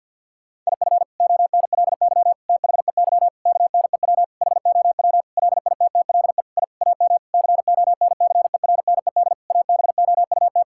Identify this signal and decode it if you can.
Morse code, heard as a keyed tone, about 50 wpm, 700 Hz. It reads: IF ONLY THEY KNEW HOW LITTLE I AM CONCERNED ABOUT
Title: Sentences from Top 1000 Words - Review Part 6 - 50wpm